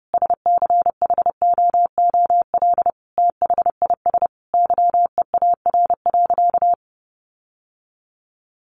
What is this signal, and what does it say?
Morse code, heard as a keyed tone, about 30 wpm, 700 Hz. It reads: SCHOOL THIS YEAR.
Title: Sentences from Top 1400 Words 30wpm